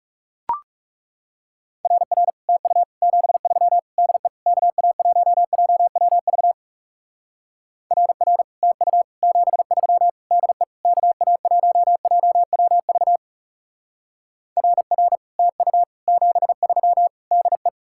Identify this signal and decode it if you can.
Morse code, heard as a keyed tone, about 39 wpm, 700 Hz. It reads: RR TU 73 DE KA1JWV RR TU 73 DE KA1JWV RR TU 73 DE